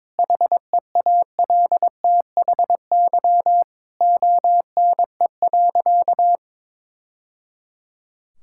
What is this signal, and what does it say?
Morse code, heard as a keyed tone, about 22 wpm, 700 Hz. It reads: HEALTHY ONE.
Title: Sentences from Top 1000 Words - Review Part 5 - 22wpm